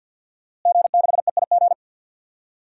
Morse code, heard as a keyed tone, about 52 wpm, 700 Hz. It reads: G6EIG
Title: DXCC-100 Call Signs - 5 Characters - Repeated 3x - 50wpm